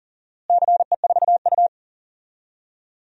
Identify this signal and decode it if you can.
Morse code, heard as a keyed tone, about 40 wpm, 700 Hz. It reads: CE4U
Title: DXCC-100 Call Signs - 4 Characters - Repeated 2x - 40wpm